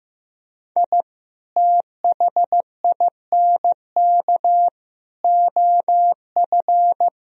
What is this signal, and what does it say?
Morse code, heard as a keyed tone, about 15 wpm, 700 Hz. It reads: I THINK OF
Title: Sentences from Top 1000 Words - Review Part 3 - 15wpm